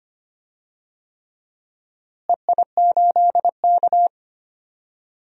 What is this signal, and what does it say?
Morse code, heard as a keyed tone, about 25 wpm, 700 Hz. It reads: EI8K
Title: DXCC-100 Call Signs - 4 Characters - Repeated 3x - 25wpm